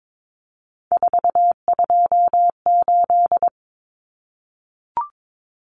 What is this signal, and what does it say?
Morse code, heard as a keyed tone, about 22 wpm, 700 Hz. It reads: 428
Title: Common Cut Numbers - 3-Digit 22wpm